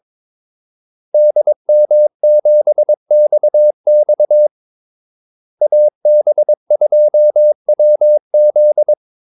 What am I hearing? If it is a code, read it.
Morse code, heard as a keyed tone, about 22 wpm, 600 Hz. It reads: DM7XX AB2WZ
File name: Callsigns-22wpm